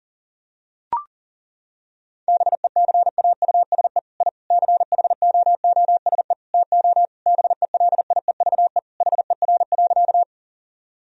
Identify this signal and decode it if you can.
Morse code, heard as a keyed tone, about 40 wpm, 700 Hz. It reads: BECAUSE I CHOOSE TO BELIEVE HER.
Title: Sentences from Top 1000 Words - Review Part 4 - 40wpm